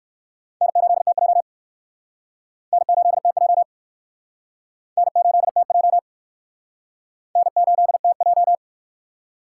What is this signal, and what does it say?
Morse code, heard as a keyed tone, about 56 wpm, 700 Hz. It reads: N8TJ N8TJ N8TJ N8TJ
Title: Speed Racing - USA 1x2 Call Signs 45wpm